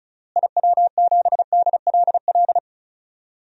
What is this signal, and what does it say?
Morse code, heard as a keyed tone, about 36 wpm, 700 Hz. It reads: IW7DLL
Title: DXCC-100 Call Signs - 6 Characters - Repeated 3x - 35wpm